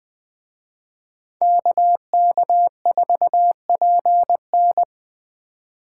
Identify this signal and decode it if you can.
Morse code, heard as a keyed tone, about 23 wpm, 700 Hz. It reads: KK4PN
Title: Speed Racing - USA 2x2 Call Signs 20wpm